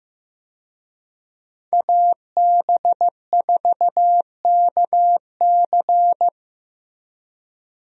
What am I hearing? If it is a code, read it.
Morse code, heard as a keyed tone, about 15 wpm, 700 Hz. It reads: AB4KC
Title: USA Call Signs - 5 Characters - Repeated 2x - 15wpm